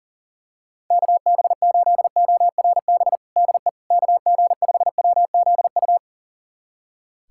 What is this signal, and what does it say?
Morse code, heard as a keyed tone, about 40 wpm, 700 Hz. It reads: KB8ORB DE KG5WZU
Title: USA Callsign DE Callsign - 6 Characters - 40wpm